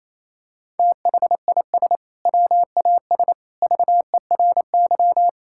Morse code, heard as a keyed tone, about 28 wpm, 700 Hz. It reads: THIS WAS VERY
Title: Sentences from Top 1000 Words - Review Part 6 - 28wpm